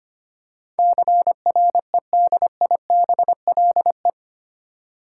Morse code, heard as a keyed tone, about 25 wpm, 700 Hz. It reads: CREDIBLE
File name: 8 Letter Words - First 3 Letters Given - 25wpm